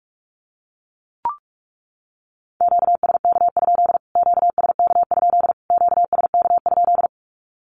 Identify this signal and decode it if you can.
Morse code, heard as a keyed tone, about 44 wpm, 700 Hz. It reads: QSK? QSK? QSK?
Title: Q-Codes Repeated 3x 45wpm